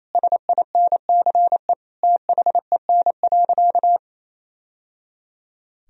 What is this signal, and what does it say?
Morse code, heard as a keyed tone, about 28 wpm, 700 Hz. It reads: SINCE THEN.
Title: Sentences from Top 1000 Words - Review Part 1 - 28wpm